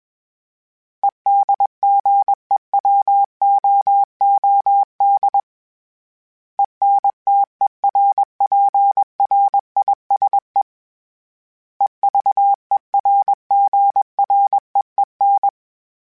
Morse code, heard as a keyed tone, about 21 wpm, 800 Hz. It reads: EDGEWOOD ENTERPRISE EVERGREEN
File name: fromSMs / FromSMs mp3s / Cities@20